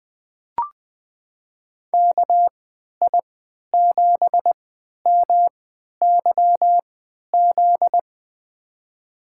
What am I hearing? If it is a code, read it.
Morse code, heard as a keyed tone, about 20 wpm, spaced as for 11 wpm, 700 Hz. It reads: KI7MYZ
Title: USA Call Signs - 3x Character Spacing - 20wpm